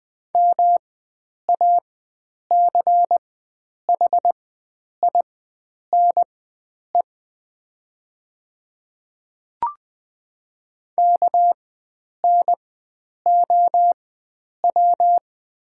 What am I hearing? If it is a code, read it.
Morse code, heard as a keyed tone, about 20 wpm, spaced as for 9 wpm, 700 Hz. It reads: MACHINE KNOW
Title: 4x Character Spacing - Top 500 Words - 20wpm